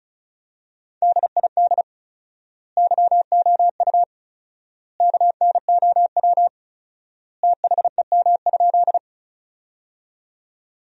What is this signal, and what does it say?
Morse code, heard as a keyed tone, about 34 wpm, 700 Hz. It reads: DID YOU KNOW THEM?
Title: 4x Word-Spacing - Sentences from Top 100 Words - 35wpm